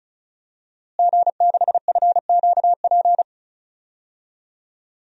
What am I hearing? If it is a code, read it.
Morse code, heard as a keyed tone, about 36 wpm, 700 Hz. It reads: G6FQP
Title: DXCC-100 Call Signs - 5 Characters - Repeated 3x - 35wpm